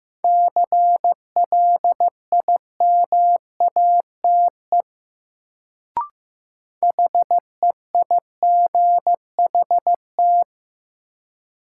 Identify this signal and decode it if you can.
Morse code, heard as a keyed tone, about 15 wpm, 700 Hz. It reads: CLIMATE HEIGHT
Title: Sentences from Top 1600 Words - New Word Drill - 15wpm